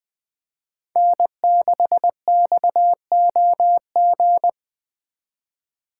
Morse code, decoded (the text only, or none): N6XOG